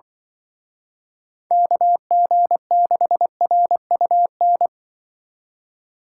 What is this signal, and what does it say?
Morse code, heard as a keyed tone, about 24 wpm, 700 Hz. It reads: KG6RUN
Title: Speed Racing - USA 2x3 Call Signs 20wpm